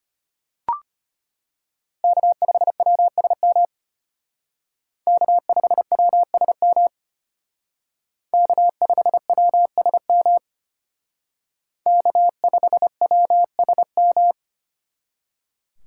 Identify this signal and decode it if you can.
Morse code, heard as a keyed tone, about 32 wpm, 700 Hz. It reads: K5WSM K5WSM K5WSM K5WSM